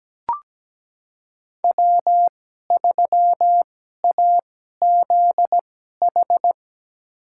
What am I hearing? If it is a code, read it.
Morse code, heard as a keyed tone, about 17 wpm, spaced as for 12 wpm, 700 Hz. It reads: W3AZH
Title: USA Call Signs - 5 Characters - 2x Character Spacing - 17wpm